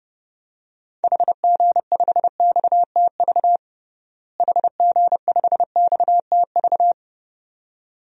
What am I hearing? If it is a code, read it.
Morse code, heard as a keyed tone, about 31 wpm, 700 Hz. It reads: HG5XTV HG5XTV
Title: DXCC-100 Call Signs - 6 Characters - Repeated 2x - 30wpm